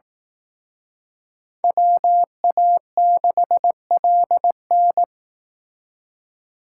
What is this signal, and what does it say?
Morse code, heard as a keyed tone, about 18 wpm, 700 Hz. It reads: WA6LN
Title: Speed Racing - USA Call Signs 15wpm